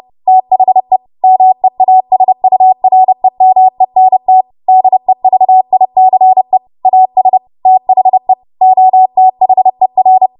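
Morse code, heard as a keyed tone, about 30 wpm, 750 Hz. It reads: THE MEASUREMENT DEVICE AS THE OTHER
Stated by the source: W1AW Web Code Practice Files, 30 WPM run from ARRL; text from April 2023 QST.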